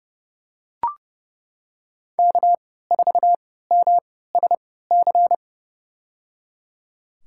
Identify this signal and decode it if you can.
Morse code, heard as a keyed tone, about 31 wpm, spaced as for 17 wpm, 700 Hz. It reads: K4MSC
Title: USA Call Signs - 3x Character Spacing - 30wpm